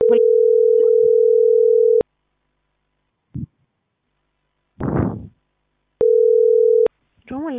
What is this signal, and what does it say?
Ringback tone (North America), 440 + 480 Hz.